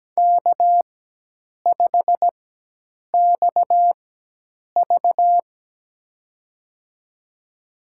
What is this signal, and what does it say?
Morse code, heard as a keyed tone, about 17 wpm, spaced as for 8 wpm, 700 Hz. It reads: K5XV
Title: USA Call Signs - 4 Characters - 4x Character Spacing - 17wpm